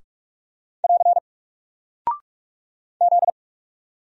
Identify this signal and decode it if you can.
Morse code, heard as a keyed tone, about 44 wpm, 700 Hz. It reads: <AR> Z
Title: Single Character (Letter:Number:Punctuation:Prosign) Alt 45wpm